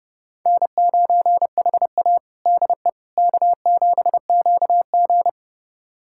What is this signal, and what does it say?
Morse code, heard as a keyed tone, about 32 wpm, 700 Hz. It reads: N9HA DE K7QG
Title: USA Callsign DE Callsign - 4 Characters - 30wpm